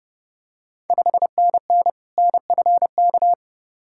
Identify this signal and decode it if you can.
Morse code, heard as a keyed tone, about 30 wpm, 700 Hz. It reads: 5NN NFK
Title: Ontario QSO Party Exchange 30wpm